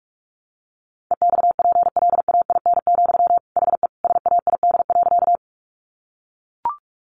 Morse code, heard as a keyed tone, about 44 wpm, 700 Hz. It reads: EXPLAIN, HE SAID.
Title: Sentences from Top 1100 Words 45wpm